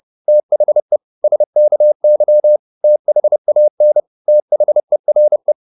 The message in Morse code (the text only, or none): THE SKY THAN THERE